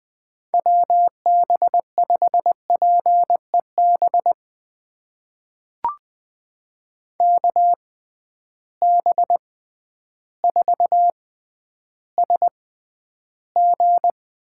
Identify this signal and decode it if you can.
Morse code, heard as a keyed tone, about 20 wpm, spaced as for 11 wpm, 700 Hz. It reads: WB5PEB KB4SG
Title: USA Call Signs - 6 Characters - 6x Character Spacing - 20wpm